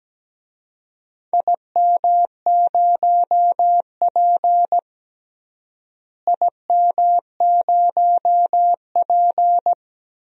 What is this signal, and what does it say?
Morse code, heard as a keyed tone, about 17 wpm, 700 Hz. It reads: IM0P IM0P
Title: DXCC-100 Call Signs - 4 Characters - Repeated 3x - 17wpm